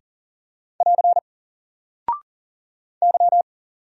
Morse code, heard as a keyed tone, about 40 wpm, 700 Hz. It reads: <AR> Y
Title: Single Character (Letter:Number:Punctuation:Prosign) 40wpm